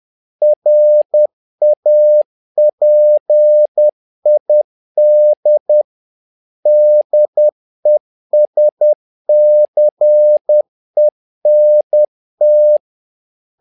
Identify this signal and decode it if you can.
Morse code, heard as a keyed tone, about 10 wpm, 600 Hz. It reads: RAPID DESCENT
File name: learning center / tower trouble / tower trouble 10wpm